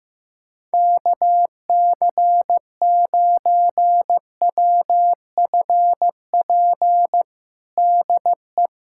Morse code, heard as a keyed tone, about 15 wpm, 700 Hz. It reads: KC9WFP DE
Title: USA Callsign DE Callsign 15wpm